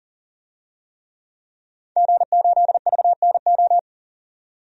Morse code, heard as a keyed tone, about 40 wpm, 700 Hz. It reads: G8VNO
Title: DXCC-100 Call Signs - 5 Characters - 40wpm